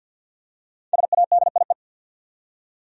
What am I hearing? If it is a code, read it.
Morse code, heard as a keyed tone, about 46 wpm, 700 Hz. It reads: SADIE
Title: USA Common Names - Max 5 Letters - 50wpm